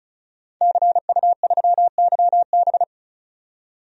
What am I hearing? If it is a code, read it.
Morse code, heard as a keyed tone, about 34 wpm, 700 Hz. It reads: CU3YB
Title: DXCC-100 Call Signs - 5 Characters - 35wpm